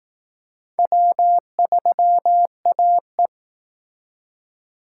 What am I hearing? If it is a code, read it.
Morse code, heard as a keyed tone, about 18 wpm, 700 Hz. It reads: W3AE
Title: Speed Racing - USA 1x2 Call Signs 15wpm